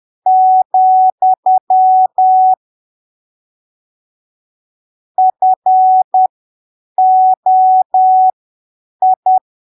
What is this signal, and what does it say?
Morse code, heard as a keyed tone, about 10 wpm, spaced as for 7 wpm, 750 Hz. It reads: , FOI
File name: arquivos / texto / texto 07 10ppm